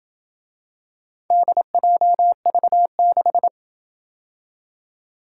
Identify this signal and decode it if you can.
Morse code, heard as a keyed tone, about 27 wpm, 700 Hz. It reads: DJV6